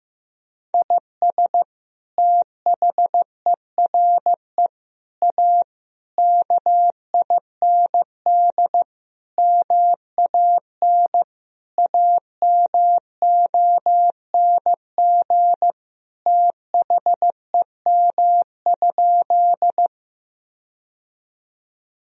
Morse code, heard as a keyed tone, about 15 wpm, 700 Hz. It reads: IS THERE A KIND MAN AMONG THEM?
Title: Sentences from Top 1000 Words - Review Part 3 - 15wpm